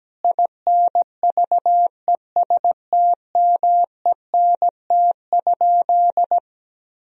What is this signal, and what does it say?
Morse code, heard as a keyed tone, about 17 wpm, 700 Hz. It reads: INVESTMENT?
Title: Sentences from Top 1000 Words - Review Part 4 - 17wpm